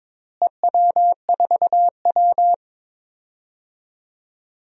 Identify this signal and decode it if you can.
Morse code, heard as a keyed tone, about 22 wpm, 700 Hz. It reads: EW4W
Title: DXCC-100 Call Signs - 4 Characters - Repeated 3x - 22wpm